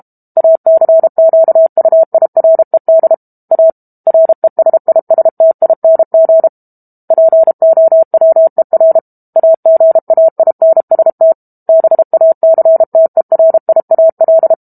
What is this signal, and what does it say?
Morse code, heard as a keyed tone, about 32 wpm, 650 Hz. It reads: ACQUIRED A RESISTING POWER AGAINST BACTERIAL
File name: Book Two / 06-Chapter 6 - 32 WPM